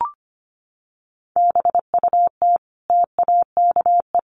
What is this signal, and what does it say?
Morse code, heard as a keyed tone, about 25 wpm, 700 Hz. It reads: BUT TAKE
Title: Sentences from Top 300 Words 25wpm